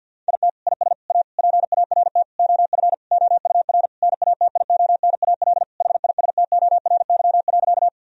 Morse code, heard as a keyed tone, about 50 wpm, 700 Hz. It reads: IT IS A PART OF OUR NATIONAL HISTORY.
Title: Sentences from Top 1000 Words - Review Part 4 - 50wpm